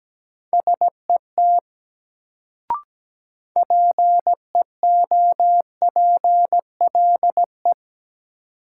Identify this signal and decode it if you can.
Morse code, heard as a keyed tone, about 17 wpm, 700 Hz. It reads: SET PEOPLE
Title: Sentences from Top 200 Words - New Word Drill - 17wpm